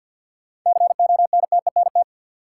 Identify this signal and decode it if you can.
Morse code, heard as a keyed tone, about 53 wpm, 700 Hz. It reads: CONTENT
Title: Sentences from Top 900 Words - New Word Drill - 50wpm